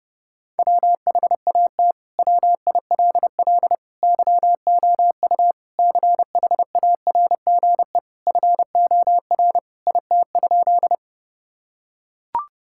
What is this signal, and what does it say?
Morse code, heard as a keyed tone, about 30 wpm, 700 Hz. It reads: WHAT WILL YOU CHARGE FOR IT?
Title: Sentences from Top 1000 Words - Review Part 6 - 30wpm